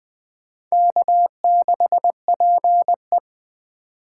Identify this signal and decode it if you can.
Morse code, heard as a keyed tone, about 24 wpm, 700 Hz. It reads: K6PE
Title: Speed Racing - USA 1x2 Call Signs 20wpm